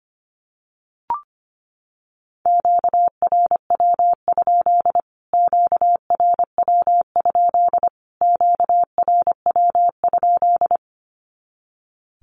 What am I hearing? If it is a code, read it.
Morse code, heard as a keyed tone, about 25 wpm, 700 Hz. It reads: QRW? QRW? QRW?
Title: Q-Codes Repeated 3x 25wpm